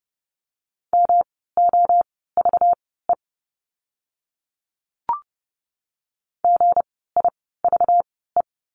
Morse code, heard as a keyed tone, about 30 wpm, spaced as for 17 wpm, 700 Hz. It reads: MOVE GIVE
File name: course / 30:10wpm / 237 Lesson 38f - Letter q - Words 30wpm